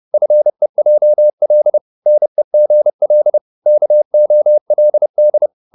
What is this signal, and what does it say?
Morse code, heard as a keyed tone, about 30 wpm, 600 Hz. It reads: FEJL NEGL KOLD
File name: Korte ord DK 30wpm